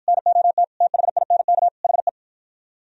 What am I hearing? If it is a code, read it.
Morse code, heard as a keyed tone, about 48 wpm, 700 Hz. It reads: NOT THINK HE